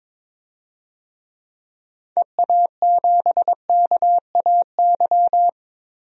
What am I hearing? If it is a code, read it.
Morse code, heard as a keyed tone, about 22 wpm, 700 Hz. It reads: EA7KAY
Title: DXCC-100 Call Signs - 6 Characters - 22wpm